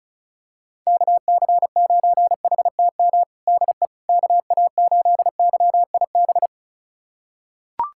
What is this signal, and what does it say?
Morse code, heard as a keyed tone, about 36 wpm, 700 Hz. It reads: KC9HTM DE KA8YIB
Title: USA Callsign DE Callsign 35wpm